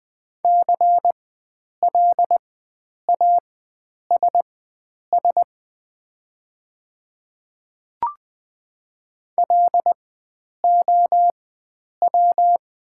Morse code, heard as a keyed tone, about 20 wpm, spaced as for 9 wpm, 700 Hz. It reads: CLASS LOW
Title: 4x Character Spacing - Top 500 Words - 20wpm